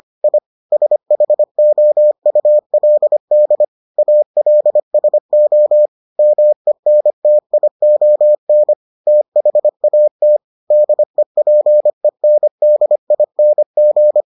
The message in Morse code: I SHOULD ALSO MENTION THAT DEPENDING

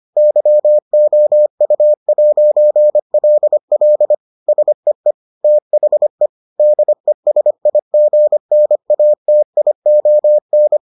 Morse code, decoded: YOU'LL SEE THE DESIGNATION

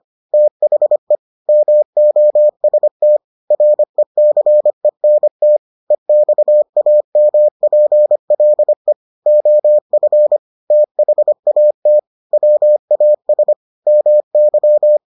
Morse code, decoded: THE MOST RECENT EXAMPLE OF THAT WAS MY

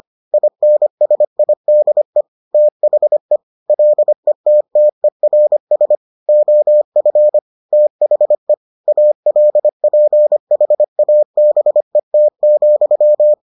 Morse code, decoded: INSIDE THE LETTERS OF THE ALPHABET,

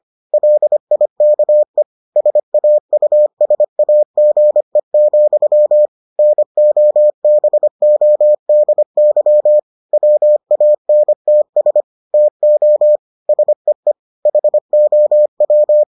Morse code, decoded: LIKE SAUSAGE, NOBODY WANTS TO SEE HOW

